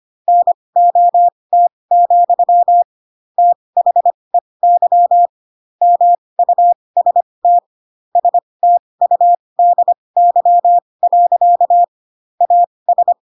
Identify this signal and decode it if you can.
Morse code, heard as a keyed tone, about 25 wpm, spaced as for 20 wpm, 700 Hz. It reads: NOT, THEY MUST STUDY. AS
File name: Pinocchio / Pinocchio ch4 15 25 ik1hge